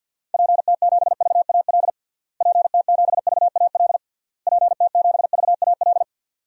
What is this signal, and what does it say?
Morse code, heard as a keyed tone, about 52 wpm, 700 Hz. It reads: PT7VAL PT7VAL PT7VAL
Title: DXCC-100 Call Signs - 6 Characters - Repeated 3x - 50wpm